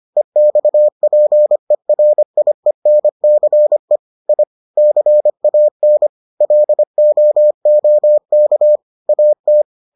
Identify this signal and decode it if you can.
Morse code, heard as a keyed tone, about 25 wpm, 600 Hz. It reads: EXPERIENCE I CAN LOOK AT